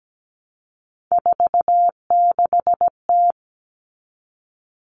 Morse code, heard as a keyed tone, about 17 wpm, 700 Hz. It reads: 46T
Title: All Cut Numbers - 3-Digit 17wpm